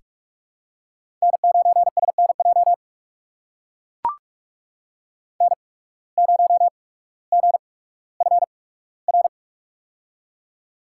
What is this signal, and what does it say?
Morse code, heard as a keyed tone, about 44 wpm, spaced as for 19 wpm, 700 Hz. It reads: N9SNJ N0GFR
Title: USA Call Signs - 5 Characters - 8x Character Spacing - 45wpm